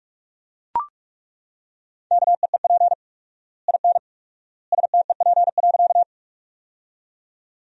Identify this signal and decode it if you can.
Morse code, heard as a keyed tone, about 42 wpm, 700 Hz. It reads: KEEP IN STEP.
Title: 4x Word-Spacing - Sentences from Top 400 Words - 45wpm